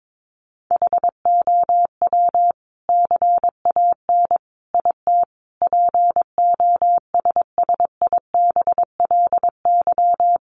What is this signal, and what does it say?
Morse code, heard as a keyed tone, about 22 wpm, 700 Hz. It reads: HOW CAN IT POSSIBLY
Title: Sentences from Top 1300 Words 22wpm